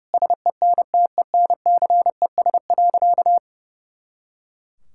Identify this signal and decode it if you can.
Morse code, heard as a keyed tone, about 30 wpm, 700 Hz. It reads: SENTENCES.
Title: Pangram Challenge v2 - 30wpm